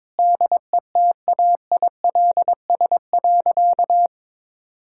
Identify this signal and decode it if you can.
Morse code, heard as a keyed tone, about 22 wpm, 700 Hz. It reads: DETAILS.
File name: Sentences with 9-Word Phrases - 22wpm